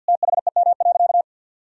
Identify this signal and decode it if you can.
Morse code, heard as a keyed tone, about 52 wpm, 700 Hz. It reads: THEM.
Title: Sentences from Top 1000 Words - Review Part 4 - 50wpm